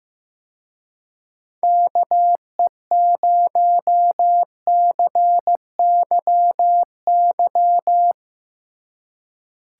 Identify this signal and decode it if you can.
Morse code, heard as a keyed tone, about 17 wpm, 700 Hz. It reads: KE0CYY
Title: Speed Racing - USA 2x3 Call Signs 15wpm